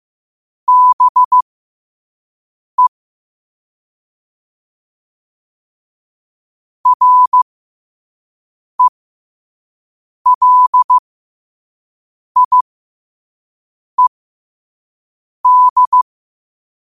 Morse code, heard as a keyed tone, about 15 wpm, spaced as for 5 wpm, 1000 Hz. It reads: BE RELIED
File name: qotdfiles / 5wpm